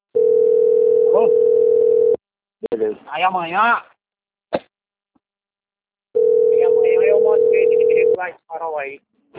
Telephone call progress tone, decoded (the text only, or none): ringback tone